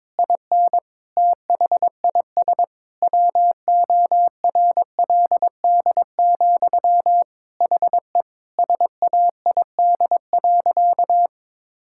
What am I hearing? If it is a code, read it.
Morse code, heard as a keyed tone, about 22 wpm, 700 Hz. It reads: IN THIS WORLD, HE SAID.
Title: Sentences from Top 1600 Words 22wpm